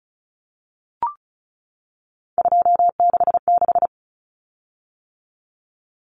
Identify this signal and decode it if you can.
Morse code, heard as a keyed tone, about 34 wpm, 700 Hz. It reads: 266
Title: Common Cut Numbers - 3-Digit 35wpm